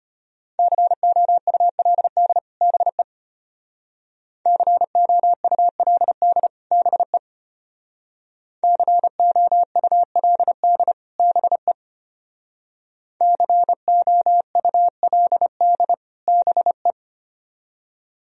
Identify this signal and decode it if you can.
Morse code, heard as a keyed tone, about 31 wpm, 700 Hz. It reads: COULD BE COULD BE COULD BE COULD BE
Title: Speed-Racing - Sets of 2 Words 25wpm